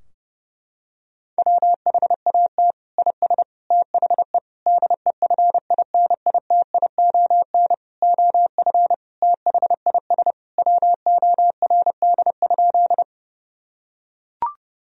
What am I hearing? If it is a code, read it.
Morse code, heard as a keyed tone, about 30 wpm, 700 Hz. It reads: WHAT IS THE DEFINITION OF THIS WORD?
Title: Sentences from Top 1000 Words - Review Part 4 - 30wpm